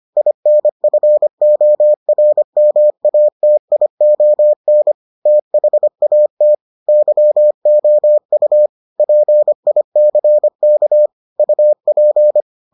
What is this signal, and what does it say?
Morse code, heard as a keyed tone, about 25 wpm, 600 Hz. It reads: INFORMATION THAT YOU PICK UP